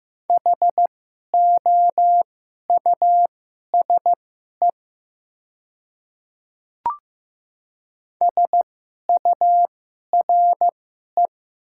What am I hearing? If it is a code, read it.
Morse code, heard as a keyed tone, about 15 wpm, spaced as for 11 wpm, 700 Hz. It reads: HOUSE SURE
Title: 2x Character Spacing - Top 500 Words - 15wpm